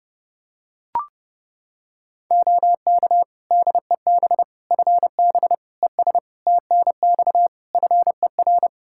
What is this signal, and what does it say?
Morse code, heard as a keyed tone, about 30 wpm, 700 Hz. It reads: OK DEB FB ES TNX FER
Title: Practice QSO - Exchange 2 - Info - 30wpm